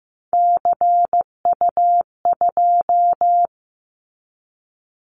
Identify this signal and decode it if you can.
Morse code, heard as a keyed tone, about 15 wpm, 700 Hz. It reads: CU2
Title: Callsign Prefixes for Top 100 DXCC - 15wpm